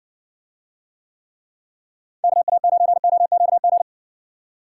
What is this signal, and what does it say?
Morse code, heard as a keyed tone, about 55 wpm, 700 Hz. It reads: KA9OQG